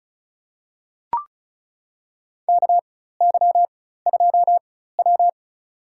Morse code, heard as a keyed tone, about 34 wpm, spaced as for 16 wpm, 700 Hz. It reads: KY2W
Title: USA Call Signs - 4 Characters - 4x Character Spacing - 35wpm